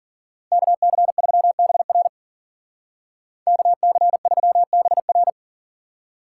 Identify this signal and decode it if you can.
Morse code, heard as a keyed tone, about 52 wpm, 700 Hz. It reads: KC3BR KC3BR